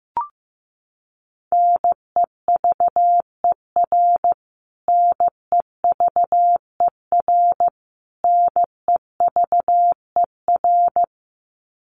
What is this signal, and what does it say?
Morse code, heard as a keyed tone, about 15 wpm, 700 Hz. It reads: NEVER NEVER NEVER
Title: Top 500 Words Repeated 15wpm